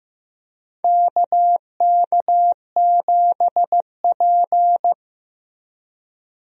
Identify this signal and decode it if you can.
Morse code, heard as a keyed tone, about 15 wpm, 700 Hz. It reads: KK7P
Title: USA Call Signs - 4 Characters - Repeated 2x - 15wpm